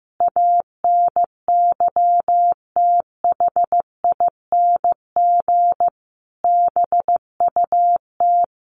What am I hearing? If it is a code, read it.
Morse code, heard as a keyed tone, about 15 wpm, 700 Hz. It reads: ANYTHING BUT
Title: Sentences from Top 1200 Words 15wpm